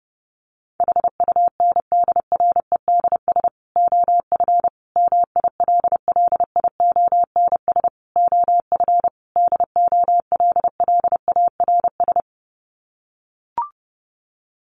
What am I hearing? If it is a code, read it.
Morse code, heard as a keyed tone, about 30 wpm, 700 Hz. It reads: HUNDREDS OF MILLIONS OF DOLLARS
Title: Sets of 5 Words 30wpm